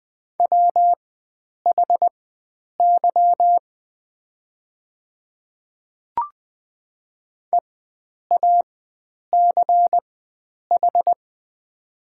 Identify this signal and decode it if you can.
Morse code, heard as a keyed tone, about 20 wpm, spaced as for 9 wpm, 700 Hz. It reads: WHY EACH
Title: 4x Character Spacing - Top 300 Words - 20wpm